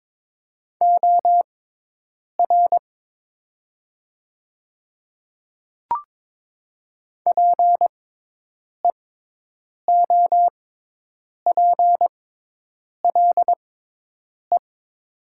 Morse code, heard as a keyed tone, about 22 wpm, spaced as for 8 wpm, 700 Hz. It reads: OR PEOPLE
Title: 6x Character Spacing - Top 100 Words - 22wpm